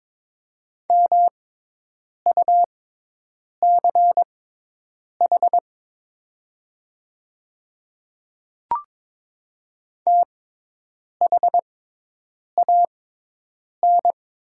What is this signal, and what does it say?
Morse code, heard as a keyed tone, about 22 wpm, spaced as for 8 wpm, 700 Hz. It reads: MUCH THAN
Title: 6x Character Spacing - Top 200 Words - 22wpm